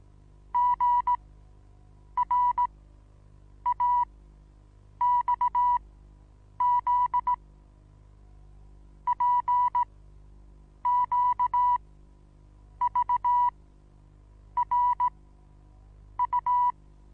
Morse code, heard as a keyed tone, about 17 wpm, spaced as for 7 wpm, 980 Hz. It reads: GRAXZ PQVRU
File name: Onlinelehrgang / morsen / DJ4UF Morsen25